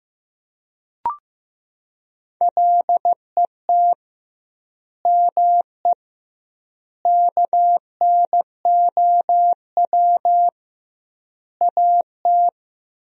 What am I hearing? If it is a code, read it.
Morse code, heard as a keyed tone, about 16 wpm, 700 Hz. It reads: LET ME KNOW AT
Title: 2x Word-Spacing - Sentences from Top 300 Words - 15wpm